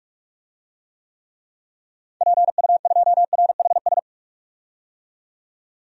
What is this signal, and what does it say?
Morse code, heard as a keyed tone, about 43 wpm, 700 Hz. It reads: PU2RHS
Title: DXCC-100 Call Signs - 6 Characters - 45wpm